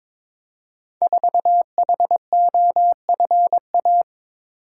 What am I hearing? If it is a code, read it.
Morse code, heard as a keyed tone, about 22 wpm, 690 Hz. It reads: 4HOFA
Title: Code Groups - 5 Letter:Numbers - 22wpm